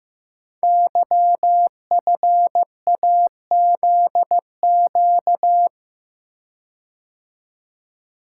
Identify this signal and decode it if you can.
Morse code, heard as a keyed tone, about 15 wpm, 700 Hz. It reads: YFAZQ
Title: Code Groups - 5 Letter:Numbers - 15wpm